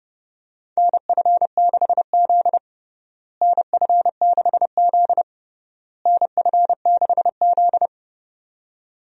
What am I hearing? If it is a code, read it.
Morse code, heard as a keyed tone, about 30 wpm, 700 Hz. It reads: NF6Z NF6Z NF6Z
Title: USA Call Signs - 4 Characters - Repeated 3x - 30wpm